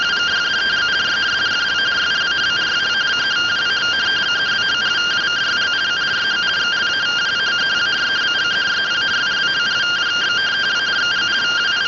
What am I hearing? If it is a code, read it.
Two-tone teleprinter signal, about 170 Hz shift.